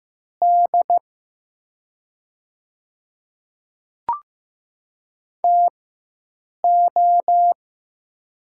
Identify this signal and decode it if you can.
Morse code, heard as a keyed tone, about 15 wpm, spaced as for 7 wpm, 700 Hz. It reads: D TO
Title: 4x Character Spacing - Top 200 Words - 15wpm